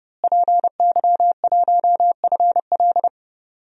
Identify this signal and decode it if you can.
Morse code, heard as a keyed tone, about 31 wpm, 700 Hz. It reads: PY1FL
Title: DXCC-100 Call Signs - 5 Characters - Repeated 2x - 30wpm